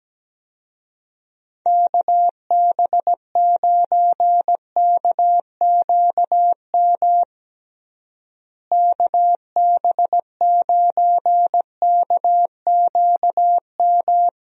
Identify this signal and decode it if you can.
Morse code, heard as a keyed tone, about 17 wpm, 700 Hz. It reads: KB9KQM KB9KQM
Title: USA Call Signs - 6 Characters - Repeated 2x - 17wpm